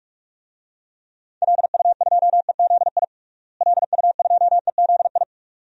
Morse code, heard as a keyed tone, about 47 wpm, 700 Hz. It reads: LU2EZI LU2EZI
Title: DXCC-100 Call Signs - 6 Characters - Repeated 2x - 45wpm